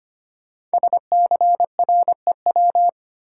Morse code, heard as a keyed tone, about 25 wpm, 700 Hz. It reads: SCREW
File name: 5 Letter Words - First 3 Letters Given - 25wpm